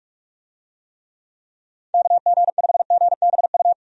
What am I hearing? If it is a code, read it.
Morse code, heard as a keyed tone, about 41 wpm, 700 Hz. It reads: KG5GBU
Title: USA Call Signs - 6 Characters - 45wpm